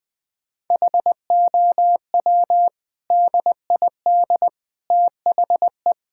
Morse code, heard as a keyed tone, about 21 wpm, 700 Hz. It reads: HOW DID THE
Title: Sentences from Top 1000 Words - Review Part 5 - 20wpm